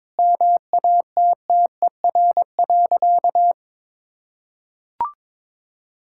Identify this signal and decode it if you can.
Morse code, heard as a keyed tone, about 22 wpm, 700 Hz. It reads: MATTER.
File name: practice / 22wpm / Sentences with 4-Word Phrases - 22wpm